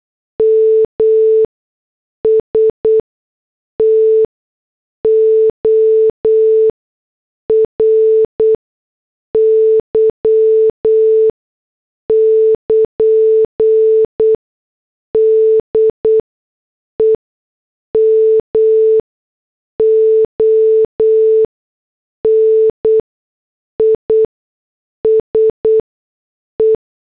Morse code, heard as a keyed tone, about 8 wpm, 440 Hz. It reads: M S T O R Y <KN> D E M O N I S E
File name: morse / headlines